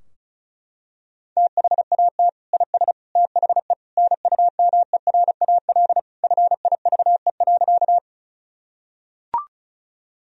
Morse code, heard as a keyed tone, about 36 wpm, 700 Hz. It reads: THAT IS THE NUMERAL FIVE.
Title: Sentences from Top 1000 Words - Review Part 2 - 35wpm